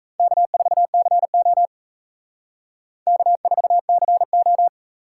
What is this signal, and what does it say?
Morse code, heard as a keyed tone, about 35 wpm, 700 Hz. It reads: K4CO K4CO
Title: Speed Racing - USA Call Signs 28wpm